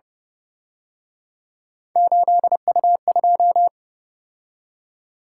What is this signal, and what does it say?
Morse code, heard as a keyed tone, about 29 wpm, 700 Hz. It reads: 8U2
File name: Speed-Racing - Three Letter-Number Combinations - 25wpm